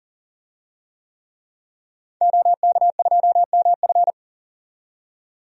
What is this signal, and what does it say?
Morse code, heard as a keyed tone, about 40 wpm, 700 Hz. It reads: OK2MF
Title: DXCC-100 Call Signs - 5 Characters - 40wpm